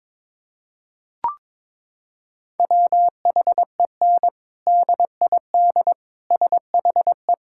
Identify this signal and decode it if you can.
Morse code, heard as a keyed tone, about 22 wpm, 700 Hz. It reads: WHEN DID SHE
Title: Sentences from Top 1000 Words - Review Part 6 - 22wpm